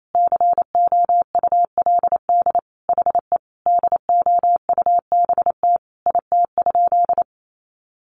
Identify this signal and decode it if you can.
Morse code, heard as a keyed tone, about 28 wpm, 700 Hz. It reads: COULD HE DOUBT IT?
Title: Sentences from Top 1500 Words 28wpm